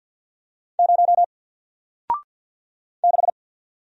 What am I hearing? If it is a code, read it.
Morse code, heard as a keyed tone, about 53 wpm, 700 Hz. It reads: 0 6
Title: Single Letter-Number Alt 50wpm